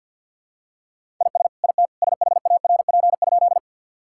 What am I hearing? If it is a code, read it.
Morse code, heard as a keyed tone, about 50 wpm, 700 Hz. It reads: IS IT SHARP?
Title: Sentences from Top 1600 Words 50wpm